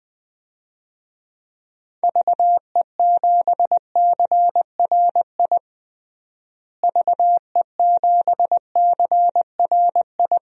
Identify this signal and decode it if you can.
Morse code, heard as a keyed tone, about 20 wpm, 700 Hz. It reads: VE7CRI VE7CRI
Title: DXCC-100 Call Signs - 6 Characters - Repeated 3x - 20wpm